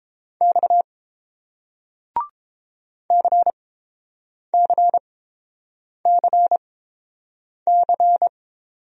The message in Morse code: X CCCC

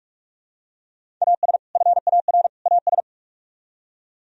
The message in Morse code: AS FAR AS

